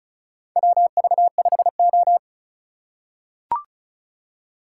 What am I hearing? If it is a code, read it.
Morse code, heard as a keyed tone, about 34 wpm, 700 Hz. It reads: WV5O
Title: USA Call Signs - 4 Characters - 4x Character Spacing - 35wpm